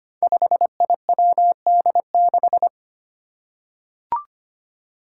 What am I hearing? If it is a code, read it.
Morse code, heard as a keyed tone, about 25 wpm, 700 Hz. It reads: 5IWD6
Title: Five Challenging Letter-Number Combinations 25wpm